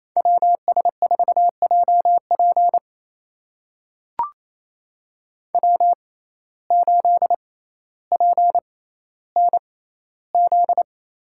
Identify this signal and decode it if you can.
Morse code, heard as a keyed tone, about 28 wpm, spaced as for 14 wpm, 700 Hz. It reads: WS4JP W8PNZ
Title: USA Call Signs - 5 Characters - 6x Character Spacing - 28wpm